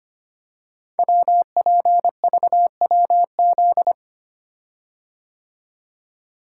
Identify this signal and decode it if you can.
Morse code, heard as a keyed tone, about 25 wpm, 700 Hz. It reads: WPVWZ
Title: Code Groups - 5 Letters - 25wpm